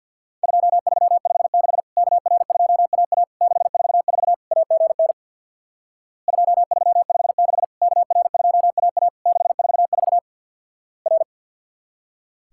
Morse code, heard as a keyed tone, about 50 wpm, 700 Hz. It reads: 2356 KR2AA 644 AGN 2356 KR2AA 644 R
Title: Worked All Europe DX Contest - QTC Exchange with Repeats - 50wpm